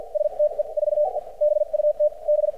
Morse code, heard as a keyed tone, about 50 wpm, spaced as for 42 wpm, 600 Hz. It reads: STE 4 BUT B
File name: archived / CRSnet